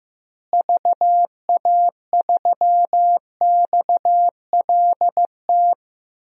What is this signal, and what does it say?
Morse code, heard as a keyed tone, about 15 wpm, 700 Hz. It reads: VA3XLT
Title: DXCC-100 Call Signs - 6 Characters - Repeated 3x - 15wpm